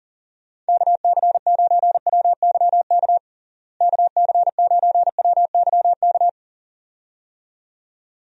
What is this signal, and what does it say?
Morse code, heard as a keyed tone, about 40 wpm, 700 Hz. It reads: KC9WYK KC9WYK
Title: USA Call Signs - 6 Characters - Repeated 2x - 40wpm